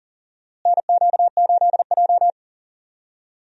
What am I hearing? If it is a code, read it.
Morse code, heard as a keyed tone, about 39 wpm, 700 Hz. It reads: NQ8J
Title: USA Call Signs - 4 Characters - Repeated 2x - 40wpm